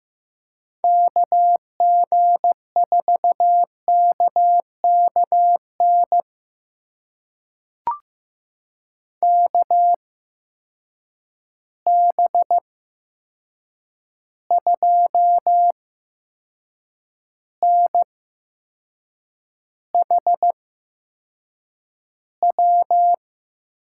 Morse code, heard as a keyed tone, about 15 wpm, spaced as for 6 wpm, 700 Hz. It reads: KG4KKN KB2NHW
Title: USA Call Signs - 6 Characters - 8x Character Spacing - 15wpm